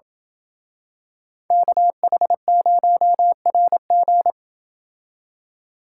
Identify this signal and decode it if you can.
Morse code, heard as a keyed tone, about 27 wpm, 700 Hz. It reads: KH0RG